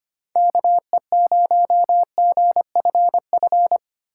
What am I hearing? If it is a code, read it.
Morse code, heard as a keyed tone, about 25 wpm, 700 Hz. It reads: KE0GFF
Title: USA Call Signs - 6 Characters - 4x Character Spacing - 25wpm